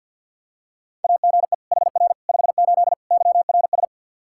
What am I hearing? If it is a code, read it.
Morse code, heard as a keyed tone, about 53 wpm, 700 Hz. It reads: AGE HR 58 YRS
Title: Practice QSO - Exchange 3 - More Info - 50wpm